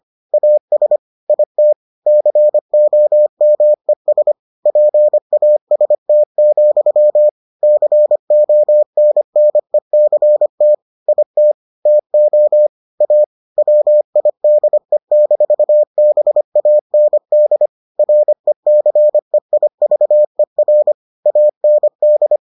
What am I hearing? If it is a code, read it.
Morse code, heard as a keyed tone, about 25 wpm, 600 Hz. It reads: AS IT COMES PAST, CONNECT IT TO A WIDE-BAND RECEIVER AND